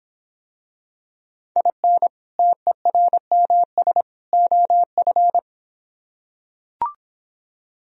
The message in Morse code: IN TERMS OF

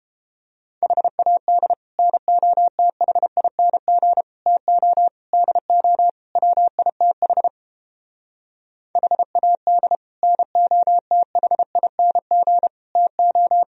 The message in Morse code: HAD NOTHING TO DO WITH HAD NOTHING TO

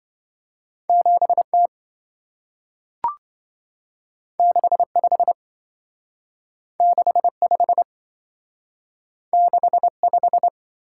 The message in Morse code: 7T 65 65 65

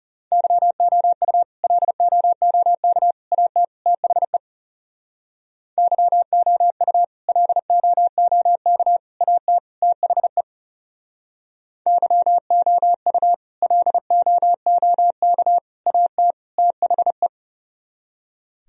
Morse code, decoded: YOU LOOK AT THE YOU LOOK AT THE YOU LOOK AT THE

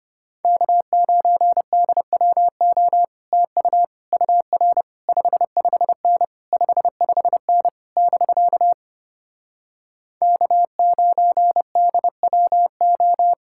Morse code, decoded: K9DWO TU UR 55N 55N <BK> K9DWO

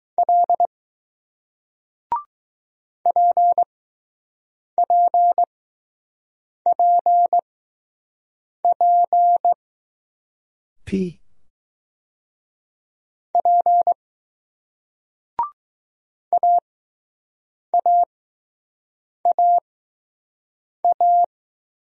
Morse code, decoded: L PPPP P AAAA

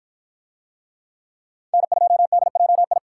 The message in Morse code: N2DJI

